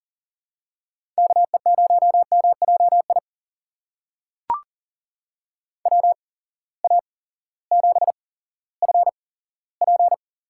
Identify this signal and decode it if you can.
Morse code, heard as a keyed tone, about 40 wpm, spaced as for 18 wpm, 700 Hz. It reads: KE0MJI WA7FP
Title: USA Call Signs - 6 Characters - 8x Character Spacing - 40wpm